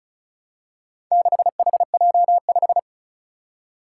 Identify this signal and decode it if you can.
Morse code, heard as a keyed tone, about 36 wpm, 700 Hz. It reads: 6HJ5
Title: Four Challenging Letter-Number Combinations 35wpm